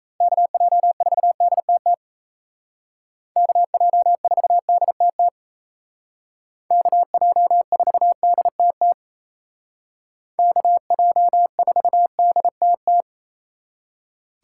Morse code, heard as a keyed tone, about 33 wpm, 700 Hz. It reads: KJ4DTT KJ4DTT KJ4DTT KJ4DTT
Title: Speed Racing - USA Call Signs 28wpm